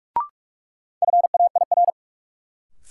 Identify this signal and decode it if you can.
Morse code, heard as a keyed tone, about 42 wpm, 700 Hz. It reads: FAIR
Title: Sentences from Top 1300 Words - New Word Drill Alt - 45wpm